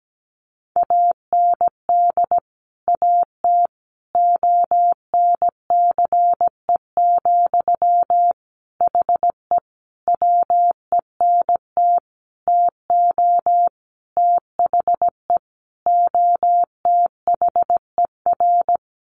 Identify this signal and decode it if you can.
Morse code, heard as a keyed tone, about 17 wpm, 700 Hz. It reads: AND AT ONCE, HE WENT TO THE OTHER
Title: Sentences from Top 1300 Words 17wpm